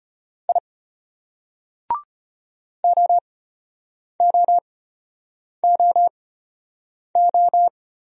Morse code, heard as a keyed tone, about 32 wpm, spaced as for 8 wpm, 700 Hz. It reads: I OOOO